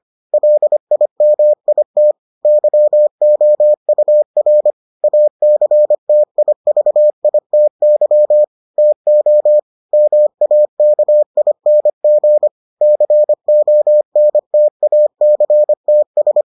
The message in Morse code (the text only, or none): LIMIT YOUR ACTIVITY TO MAKING CONTACTS